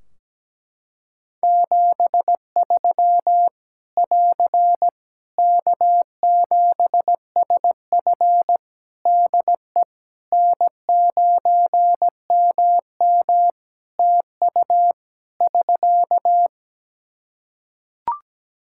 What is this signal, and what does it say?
Morse code, heard as a keyed tone, about 17 wpm, 700 Hz. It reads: 73 <AR> K7SF DE N9MM TU <SK>
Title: K1USN SST QSO Practice 17wpm